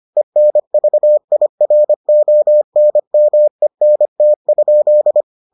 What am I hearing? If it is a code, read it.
Morse code, heard as a keyed tone, about 25 wpm, 600 Hz. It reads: ENVIRONMENT?